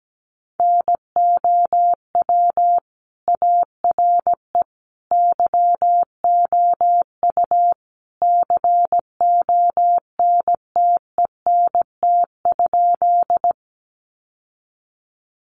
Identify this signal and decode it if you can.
Morse code, heard as a keyed tone, about 17 wpm, 700 Hz. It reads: NOW ARE YOU CONTENT?
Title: Sentences from Top 900 Words 17wpm